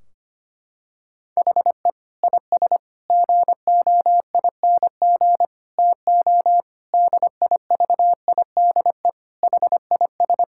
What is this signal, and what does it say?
Morse code, heard as a keyed tone, about 25 wpm, 700 Hz. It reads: HE IS GOING TO DIVIDE HIS
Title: Sentences from Top 1600 Words 25wpm